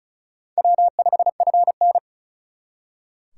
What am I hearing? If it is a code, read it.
Morse code, heard as a keyed tone, about 45 wpm, 700 Hz. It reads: W5FN